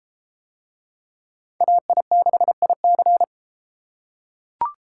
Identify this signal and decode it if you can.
Morse code, heard as a keyed tone, about 29 wpm, 700 Hz. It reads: AI6IC